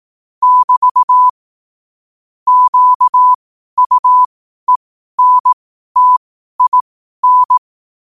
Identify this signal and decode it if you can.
Morse code, heard as a keyed tone, about 18 wpm, spaced as for 12 wpm, 1000 Hz. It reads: <BT> QUENTIN
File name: qotdfiles / 12wpm